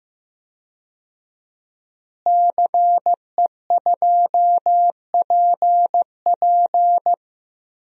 Morse code, heard as a keyed tone, about 16 wpm, 700 Hz. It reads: CE2PP
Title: DXCC-100 Call Signs - 5 Characters - Repeated 3x - 15wpm